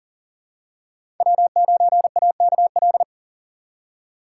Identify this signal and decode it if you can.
Morse code, heard as a keyed tone, about 39 wpm, 700 Hz. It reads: W9AKL
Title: USA Call Signs - 5 Characters - 3x Character Spacing - 40wpm